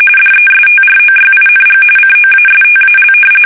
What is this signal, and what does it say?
RTTY, about 45.45 baud. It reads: DESTROYING COASTAL
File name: fsk850 / battleshipnj